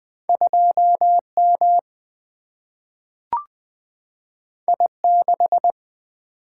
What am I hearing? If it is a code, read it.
Morse code, heard as a keyed tone, about 20 wpm, 700 Hz. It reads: 2 M I 6
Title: Single-Letter Single-Number 20wpm